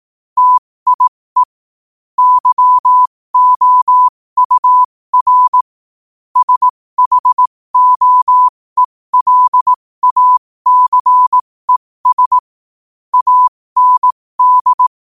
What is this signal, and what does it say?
Morse code, heard as a keyed tone, about 18 wpm, spaced as for 15 wpm, 1000 Hz. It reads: TIE YOUR SHOELACES AND
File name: qotdfiles / 15wpm